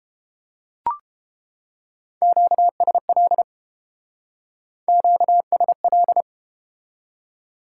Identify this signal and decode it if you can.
Morse code, heard as a keyed tone, about 31 wpm, 700 Hz. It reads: QSL QSL